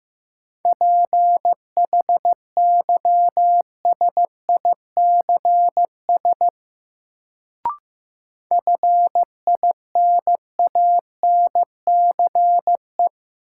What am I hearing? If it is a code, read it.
Morse code, heard as a keyed tone, about 15 wpm, 700 Hz. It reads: PHYSICS FINANCE
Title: Sentences from Top 1500 Words - New Word Drill Alt - 15wpm